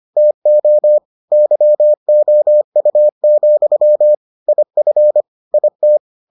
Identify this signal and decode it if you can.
Morse code, heard as a keyed tone, about 25 wpm, 600 Hz. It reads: TO YOU, IF IT